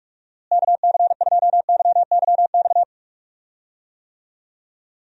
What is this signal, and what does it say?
Morse code, heard as a keyed tone, about 46 wpm, 700 Hz. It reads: KC2YYX
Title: USA Call Signs - 6 Characters - Repeated 2x - 45wpm